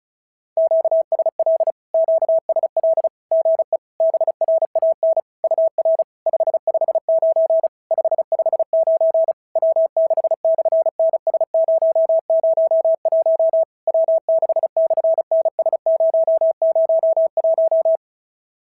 Morse code, heard as a keyed tone, about 36 wpm, 650 Hz. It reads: QSL QSL GE BRAN UR 559 559 W6/NS001 W6/NS001
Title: SOTA QSO Practice 35wpm